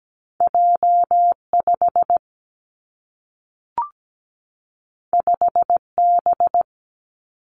Morse code, heard as a keyed tone, about 17 wpm, 700 Hz. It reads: J 5 5 B
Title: Two Challenging Letter-Number Combinations 17wpm